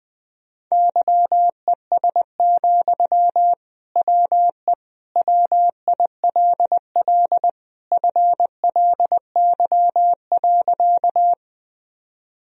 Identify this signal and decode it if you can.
Morse code, heard as a keyed tone, about 20 wpm, 700 Hz. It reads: YES, WE WILL FLY.
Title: Sentences from Top 1000 Words - Review Part 2 - 20wpm